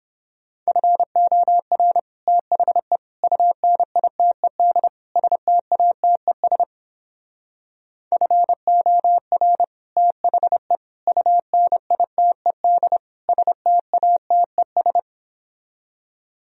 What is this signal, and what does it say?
Morse code, heard as a keyed tone, about 29 wpm, 700 Hz. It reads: FOR THE UNITED STATES FOR THE UNITED STATES